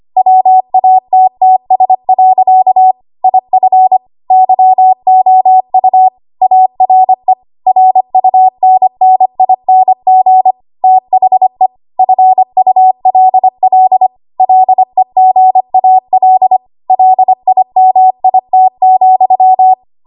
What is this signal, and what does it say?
Morse code, heard as a keyed tone, about 25 wpm, 750 Hz. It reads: WATTS. IF YOU ARE RUNNING THE FULL LEGAL LIMIT,
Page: W1AW Web Code Practice Files